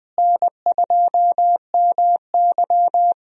Morse code, heard as a keyed tone, about 20 wpm, 700 Hz. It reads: N2MY
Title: USA Call Signs - 4 Characters - Repeated 2x - 20wpm